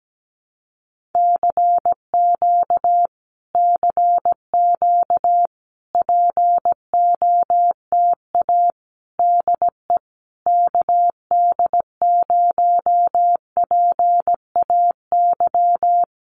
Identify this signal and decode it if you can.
Morse code, heard as a keyed tone, about 17 wpm, 700 Hz. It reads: CQ CQ POTA DE KD0PAY
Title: Common QSO Phrases Encore 17wpm